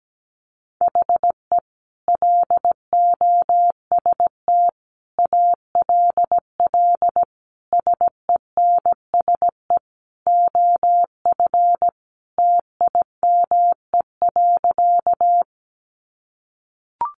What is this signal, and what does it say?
Morse code, heard as a keyed tone, about 17 wpm, 700 Hz. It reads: HE LOST ALL SENSE OF TIME.
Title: Sentences from Top 1400 Words 17wpm